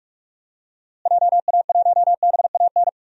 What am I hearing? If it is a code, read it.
Morse code, heard as a keyed tone, about 46 wpm, 700 Hz. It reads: JA1BAN
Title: DXCC-100 Call Signs - 6 Characters - Repeated 2x - 45wpm